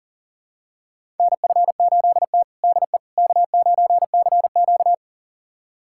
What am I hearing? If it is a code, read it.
Morse code, heard as a keyed tone, about 37 wpm, 700 Hz. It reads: NF8T DE K9CQ
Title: USA Callsign DE Callsign - 4 Characters - 40wpm